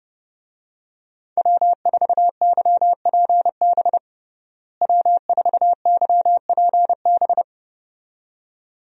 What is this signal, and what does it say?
Morse code, heard as a keyed tone, about 29 wpm, 700 Hz. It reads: W4YPB W4YPB
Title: USA Call Signs - 5 Characters - Repeated 2x - 30wpm